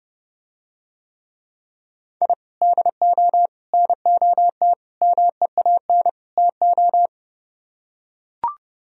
Morse code, heard as a keyed tone, about 30 wpm, 700 Hz. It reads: I DO NOT MEAN TO